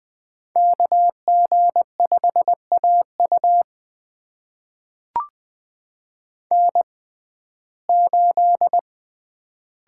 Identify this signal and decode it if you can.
Morse code, heard as a keyed tone, about 20 wpm, spaced as for 14 wpm, 700 Hz. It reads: KG5AU N8
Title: USA Call Signs - 5 Characters - 6x Character Spacing - 20wpm